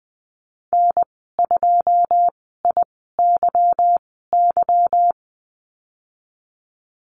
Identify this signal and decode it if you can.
Morse code, heard as a keyed tone, about 20 wpm, spaced as for 14 wpm, 700 Hz. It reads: N2IYY
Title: 161 Lesson 26g - Letter g - Call Signs 20wpm